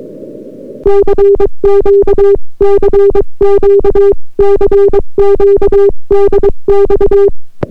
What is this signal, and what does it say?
Morse code, heard as a keyed tone, about 22 wpm, 400 Hz. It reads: CQ CQ CQ DX